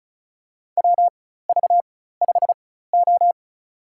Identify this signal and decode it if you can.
Morse code, heard as a keyed tone, about 34 wpm, spaced as for 16 wpm, 700 Hz. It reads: WV5O
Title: USA Call Signs - 4 Characters - 4x Character Spacing - 35wpm